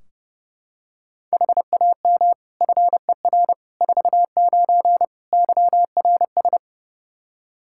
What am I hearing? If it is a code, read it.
Morse code, heard as a keyed tone, about 32 wpm, 700 Hz. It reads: HAM FER 49 YRS
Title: K1USN SST QSO Practice 30wpm